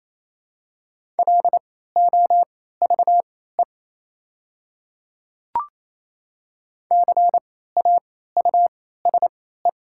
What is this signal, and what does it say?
Morse code, heard as a keyed tone, about 27 wpm, spaced as for 16 wpm, 700 Hz. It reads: LOVE CAUSE
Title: 3x Character Spacing - Top 500 Words - 28wpm